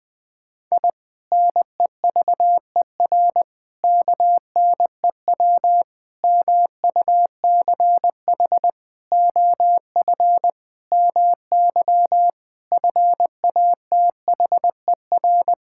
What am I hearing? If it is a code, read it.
Morse code, heard as a keyed tone, about 20 wpm, 700 Hz. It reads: I NEVER KNEW MUCH OF MY FATHER
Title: Sentences from Top 1000 Words - Review Part 6 - 20wpm